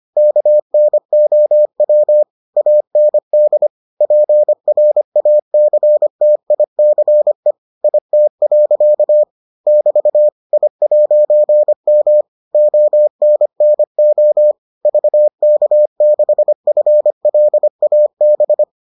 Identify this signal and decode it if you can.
Morse code, heard as a keyed tone, about 25 wpm, 600 Hz. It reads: KNOW AND PRACTICE IT. <BT> I'M ONNO VK6FLAB